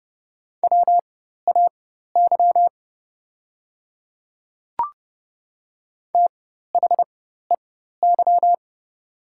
Morse code, heard as a keyed tone, about 30 wpm, spaced as for 14 wpm, 700 Hz. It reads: WAY THEY
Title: 4x Character Spacing - Top 100 Words - 30wpm